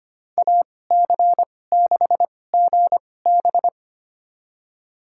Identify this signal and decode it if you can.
Morse code, heard as a keyed tone, about 25 wpm, spaced as for 18 wpm, 700 Hz. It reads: AC6GB
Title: USA Call Signs - 2x Character Spacing - 25wpm